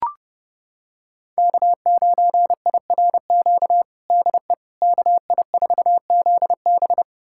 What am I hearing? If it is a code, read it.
Morse code, heard as a keyed tone, about 32 wpm, 700 Hz. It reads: K9IRQ DE KI4ZB
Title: USA Callsign DE Callsign - 5 Characters - 30wpm